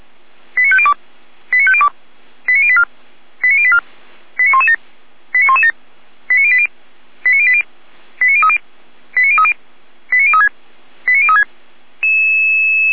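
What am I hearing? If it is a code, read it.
Five-tone selective call (ZVEI), heard as quick runs of five tones, tones of about 70 ms. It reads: 89502 89502 89084 89084 89117 89117 89080 89080 89030 89030 89036 89036
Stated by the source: ZVEI five-tone selective call received on SDR, fire brigade Feuerwehr Orlamünde.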